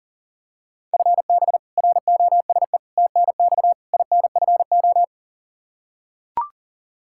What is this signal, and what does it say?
Morse code, heard as a keyed tone, about 37 wpm, 700 Hz. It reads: FB ROSE TNX INFO
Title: Practice QSO - Exchange 3 - More Info - 40wpm